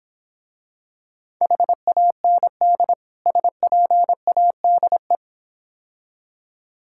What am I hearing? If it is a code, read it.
Morse code, heard as a keyed tone, about 26 wpm, 700 Hz. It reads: HAND SPADE